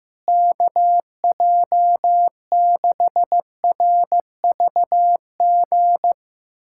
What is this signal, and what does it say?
Morse code, heard as a keyed tone, about 15 wpm, 700 Hz. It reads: KJ6RVG
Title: USA Callsign DE Callsign - 6 Characters - 15wpm